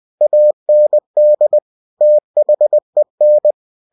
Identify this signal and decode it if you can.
Morse code, heard as a keyed tone, about 20 wpm, 600 Hz. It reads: AND THEN